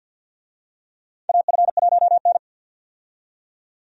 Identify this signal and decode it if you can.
Morse code, heard as a keyed tone, about 47 wpm, 700 Hz. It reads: AF1N
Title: USA Call Signs - 4 Characters - 50wpm